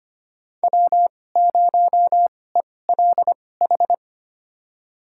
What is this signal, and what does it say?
Morse code, heard as a keyed tone, about 25 wpm, spaced as for 18 wpm, 700 Hz. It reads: W0ELH
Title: USA Call Signs - 2x Character Spacing - 25wpm